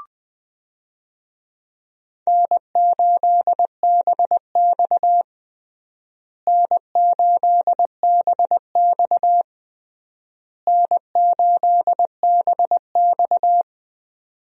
Morse code, heard as a keyed tone, about 20 wpm, 700 Hz. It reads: N8BX N8BX N8BX
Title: USA Call Signs - 4 Characters - Repeated 3x - 20wpm